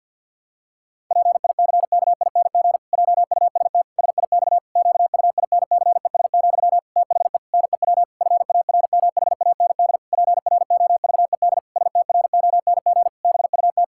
Morse code, heard as a keyed tone, about 50 wpm, 700 Hz. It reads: PICKING JUST SIX QUINCES, THE NEW FARMHAND PROVED STRONG BUT
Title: Pangram Challenge v2 - 50wpm